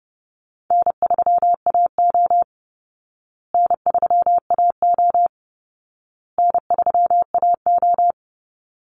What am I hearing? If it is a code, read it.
Morse code, heard as a keyed tone, about 30 wpm, 700 Hz. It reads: N3AO N3AO N3AO
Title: Easy Call Signs Repeated 3x - 30wpm